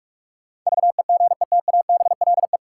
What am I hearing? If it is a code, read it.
Morse code, heard as a keyed tone, about 46 wpm, 700 Hz. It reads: VEGETABLE
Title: Sentences from Top 1300 Words - New Word Drill Alt - 45wpm